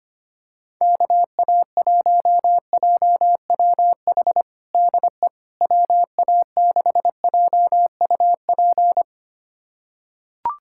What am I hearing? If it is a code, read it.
Morse code, heard as a keyed tone, about 25 wpm, 700 Hz. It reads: KA1JWH DE WA6JUP
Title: USA Callsign DE Callsign - 6 Characters - 25wpm